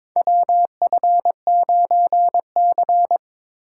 Morse code, heard as a keyed tone, about 22 wpm, 700 Hz. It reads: WF9C
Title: USA Call Signs - 4 Characters - 2x Character Spacing - 22wpm